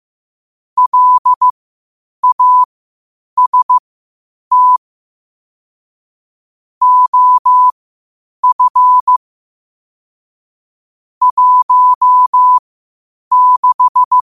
Morse code, heard as a keyed tone, about 15 wpm, spaced as for 8 wpm, 1000 Hz. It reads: LAST OF 16
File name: qotdfiles / 8wpm